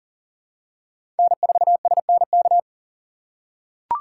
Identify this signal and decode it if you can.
Morse code, heard as a keyed tone, about 40 wpm, 700 Hz. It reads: N4SNK
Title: USA Call Signs - 5 Characters - 4x Character Spacing - 40wpm